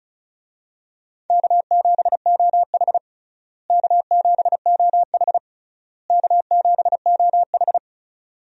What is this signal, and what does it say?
Morse code, heard as a keyed tone, about 36 wpm, 700 Hz. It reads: K7OH K7OH K7OH
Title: USA Call Signs - 4 Characters - Repeated 3x - 35wpm